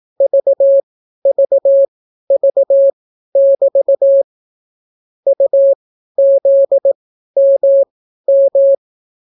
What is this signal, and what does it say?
Morse code, heard as a keyed tone, about 18 wpm, spaced as for 12 wpm, 550 Hz. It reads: VVV<BT> UZMM